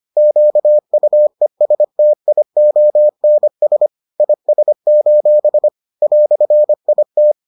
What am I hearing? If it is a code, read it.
Morse code, heard as a keyed tone, about 25 wpm, 600 Hz. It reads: QUESTIONS IS: "IT